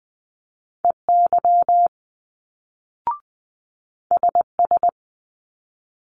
Morse code, heard as a keyed tone, about 20 wpm, 700 Hz. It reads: EY SS
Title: Multi-letter Phonemes Quick Fire 20wpm